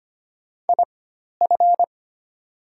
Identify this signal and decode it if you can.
Morse code, heard as a keyed tone, about 25 wpm, spaced as for 12 wpm, 700 Hz. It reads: IF
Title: 4x Character Spacing - Top 200 Words - 25wpm